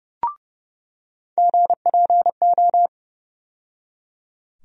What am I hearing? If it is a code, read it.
Morse code, heard as a keyed tone, about 32 wpm, 700 Hz. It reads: GPO
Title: USA Call Sign Postfixes - 3 Letters 30wpm